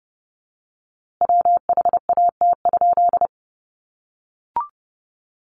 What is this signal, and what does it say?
Morse code, heard as a keyed tone, about 30 wpm, 700 Hz. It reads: WHAT?
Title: A Sound Of Thunder 30wpm